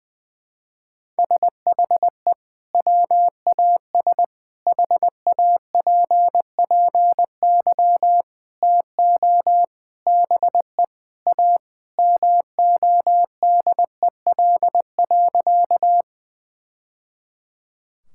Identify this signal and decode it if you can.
Morse code, heard as a keyed tone, about 21 wpm, 700 Hz. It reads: SHE WAS HAPPY TO BE A MODEL.
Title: Sentences from Top 1000 Words - Review Part 5 - 20wpm